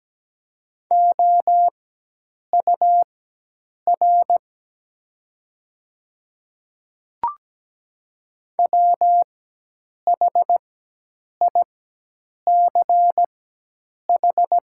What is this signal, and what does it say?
Morse code, heard as a keyed tone, about 17 wpm, spaced as for 8 wpm, 700 Hz. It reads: OUR WHICH
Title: 4x Character Spacing - Top 300 Words - 17wpm